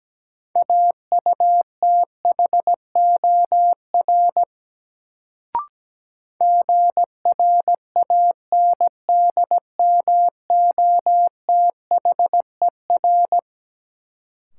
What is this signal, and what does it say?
Morse code, heard as a keyed tone, about 17 wpm, 700 Hz. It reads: AUTHOR GRANDMOTHER
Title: Sentences from Top 1200 Words - New Word Drill Alt - 17wpm